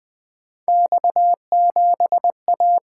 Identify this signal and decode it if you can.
Morse code, heard as a keyed tone, about 20 wpm, 700 Hz. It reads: X7A
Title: Three Letter-Number Combinations - 20wpm